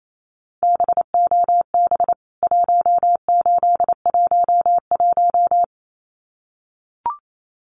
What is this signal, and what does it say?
Morse code, heard as a keyed tone, about 28 wpm, 700 Hz. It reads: BOB 1811
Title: CWT Exchange 28wpm